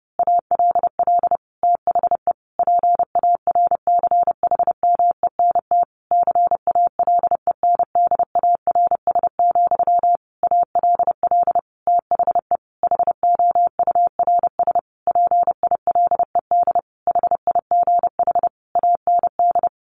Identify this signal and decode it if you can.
Morse code, heard as a keyed tone, about 30 wpm, 700 Hz. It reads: ALL THE PARCHMENT CALENDARS, ALL THE HOURS PILED HIGH AND
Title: A Sound Of Thunder 30wpm